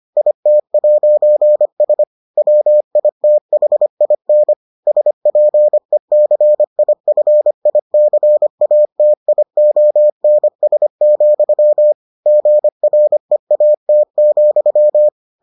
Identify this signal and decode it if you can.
Morse code, heard as a keyed tone, about 25 wpm, 600 Hz. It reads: IT'S WITHIN SPECIFICATIONS, GREAT,